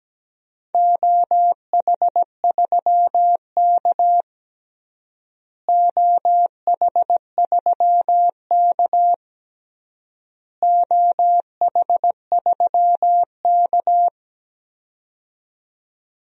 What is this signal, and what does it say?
Morse code, heard as a keyed tone, about 17 wpm, 700 Hz. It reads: OH3K OH3K OH3K
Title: DXCC-100 Call Signs - 4 Characters - Repeated 3x - 17wpm